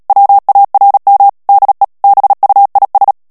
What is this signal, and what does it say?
Morse code, heard as a keyed tone, about 37 wpm, 800 Hz. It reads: WARM DE BUIS
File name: morse mp3 / HOLLANDAIS texte clair 2 vitesse 35 mots minute